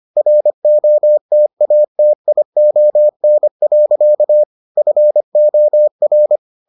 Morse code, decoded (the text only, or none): ROTATION. FOR